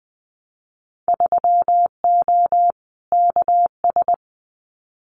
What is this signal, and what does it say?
Morse code, heard as a keyed tone, about 20 wpm, 700 Hz. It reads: 3O KS
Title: Winter Field Day Exchanges Encore 20wpm